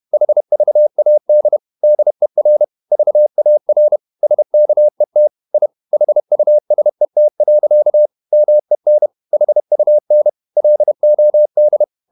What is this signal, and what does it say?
Morse code, heard as a keyed tone, about 31 wpm, 600 Hz. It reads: HVAD DER VAR SKET I HUSET. MEN HUN LOD
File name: Ferie QRQ 31wpm